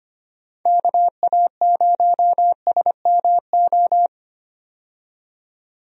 Morse code, heard as a keyed tone, about 25 wpm, 700 Hz. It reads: KA0SMO
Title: USA Call Signs - 6 Characters - 25wpm